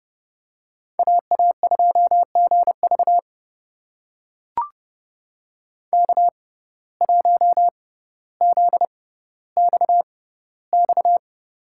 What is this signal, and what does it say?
Morse code, heard as a keyed tone, about 30 wpm, spaced as for 15 wpm, 700 Hz. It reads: AA2GV K1ZXX
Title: USA Call Signs - 5 Characters - 6x Character Spacing - 30wpm